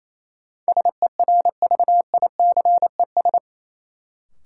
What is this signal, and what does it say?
Morse code, heard as a keyed tone, about 27 wpm, 700 Hz. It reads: SERVICES
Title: Sets of 5 Words - New Word Drill - 28wpm